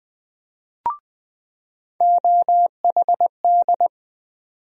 Morse code, heard as a keyed tone, about 20 wpm, 700 Hz. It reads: OHD
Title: USA Call Sign Postfixes - 3 Letters 20wpm